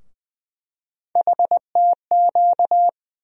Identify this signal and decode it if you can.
Morse code, heard as a keyed tone, about 20 wpm, 700 Hz. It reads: HTQ
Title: USA Call Sign Postfixes - 3 Letters 20wpm